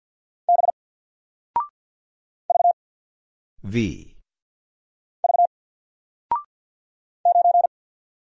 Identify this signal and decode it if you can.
Morse code, heard as a keyed tone, about 50 wpm, 700 Hz. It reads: B V V 9
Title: Single Letter-Number 50wpm